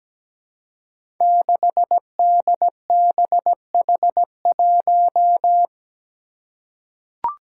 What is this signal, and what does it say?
Morse code, heard as a keyed tone, about 17 wpm, 700 Hz. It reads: 6DBH1
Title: Five Challenging Letter-Number Combinations 17wpm